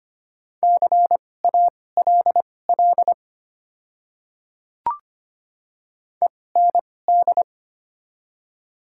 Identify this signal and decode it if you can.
Morse code, heard as a keyed tone, about 25 wpm, spaced as for 18 wpm, 700 Hz. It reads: CALL END
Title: 2x Character Spacing - Top 200 Words - 25wpm